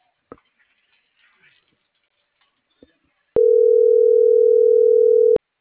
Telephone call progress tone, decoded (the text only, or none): ringback tone